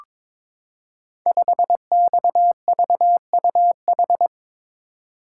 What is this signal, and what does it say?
Morse code, heard as a keyed tone, about 22 wpm, 700 Hz. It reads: 5XVUH
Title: Code Groups - 5 Letter:Numbers - 22wpm